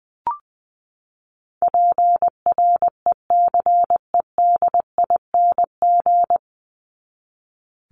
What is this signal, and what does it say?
Morse code, heard as a keyed tone, about 20 wpm, 700 Hz. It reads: PRECEDING
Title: ING Suffix - Top 100 Words with Base Words 20wpm